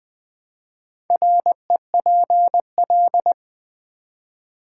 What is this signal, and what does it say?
Morse code, heard as a keyed tone, about 20 wpm, 700 Hz. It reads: REPL
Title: Build A Word - 1 Letter At A Time - 5-Letter Words - 20wpm